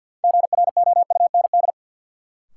Morse code, heard as a keyed tone, about 49 wpm, 700 Hz. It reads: GROUND
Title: Sets of 3 Words - New Word Drill - 50wpm